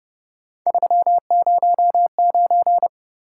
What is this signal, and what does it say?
Morse code, heard as a keyed tone, about 29 wpm, 700 Hz. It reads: 309
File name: Speed-Racing - Three Letter-Number Combinations - 25wpm